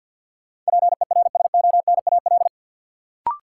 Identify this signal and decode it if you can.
Morse code, heard as a keyed tone, about 47 wpm, 700 Hz. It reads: PERSONAL
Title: Sentences from Top 700 Words - New Word Drill - 50wpm